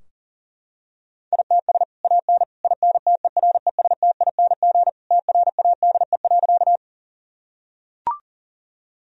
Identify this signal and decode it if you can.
Morse code, heard as a keyed tone, about 40 wpm, 700 Hz. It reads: ITS AN INTERESTING TRADE.
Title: Sentences from Top 1000 Words - Review Part 4 - 40wpm